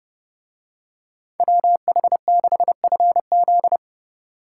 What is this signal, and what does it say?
Morse code, heard as a keyed tone, about 31 wpm, 700 Hz. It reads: WH6FZ